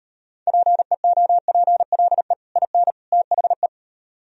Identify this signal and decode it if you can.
Morse code, heard as a keyed tone, about 34 wpm, 700 Hz. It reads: PEOPLE IN THE